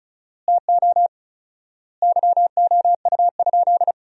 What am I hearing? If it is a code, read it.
Morse code, heard as a keyed tone, about 36 wpm, 700 Hz. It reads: TO YOU?
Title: 4x Word-Spacing - Sentences from Top 100 Words - 35wpm